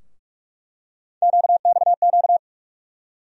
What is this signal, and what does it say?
Morse code, heard as a keyed tone, about 42 wpm, 700 Hz. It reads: QXQ
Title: USA Call Sign Postfixes - 3 Letters 45wpm